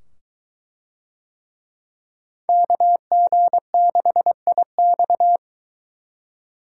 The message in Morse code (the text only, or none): KG6IX